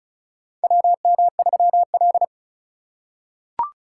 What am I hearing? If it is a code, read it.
Morse code, heard as a keyed tone, about 34 wpm, 700 Hz. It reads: WM3L
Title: USA Call Signs - 4 Characters - 2x Character Spacing - 35wpm